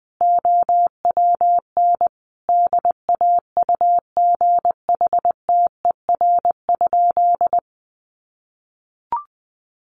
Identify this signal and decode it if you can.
Morse code, heard as a keyed tone, about 19 wpm, 700 Hz. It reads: OWN DAUGHTER?
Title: Sentences from Top 1500 Words 20wpm